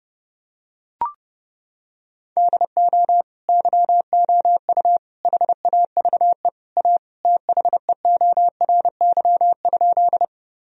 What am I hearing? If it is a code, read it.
Morse code, heard as a keyed tone, about 30 wpm, 700 Hz. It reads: DO YOU HAVE A THEORY?
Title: Sentences from Top 1000 Words - Review Part 3 - 30wpm